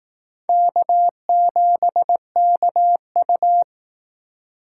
Morse code, heard as a keyed tone, about 18 wpm, 700 Hz. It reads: K7KU
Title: Speed Racing - USA Call Signs 15wpm